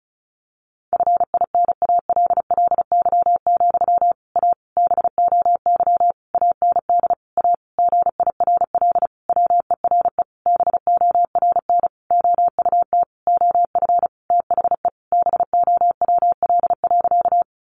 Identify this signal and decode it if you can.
Morse code, heard as a keyed tone, about 34 wpm, 700 Hz. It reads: FINALLY, A BOY AND A GIRL WERE BORN OUT OF THE BOWL.
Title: Sentences from Top 1200 Words 35wpm